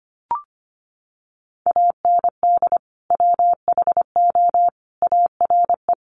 Morse code, heard as a keyed tone, about 25 wpm, 700 Hz. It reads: AND WHO ARE
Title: Sentences from Top 100 Words 25wpm